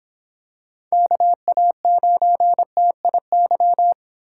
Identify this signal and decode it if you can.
Morse code, heard as a keyed tone, about 26 wpm, 700 Hz. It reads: KA9TIY